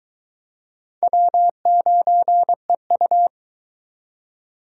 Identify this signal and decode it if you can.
Morse code, heard as a keyed tone, about 21 wpm, 700 Hz. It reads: W9EU